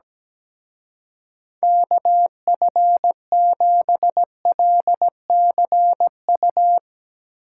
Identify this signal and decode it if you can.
Morse code, heard as a keyed tone, about 18 wpm, 700 Hz. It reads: KF7LCU
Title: Speed Racing - USA Call Signs 17wpm